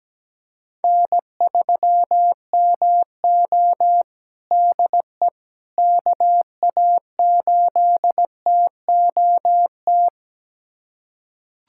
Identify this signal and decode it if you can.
Morse code, heard as a keyed tone, about 17 wpm, 700 Hz. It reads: N3MO DE KA8TOT
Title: USA Callsign DE Callsign 17wpm